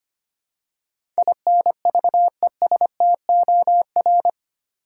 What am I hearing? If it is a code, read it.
Morse code, heard as a keyed tone, about 25 wpm, 700 Hz. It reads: INVESTOR
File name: Eight Letter Words - 25wpm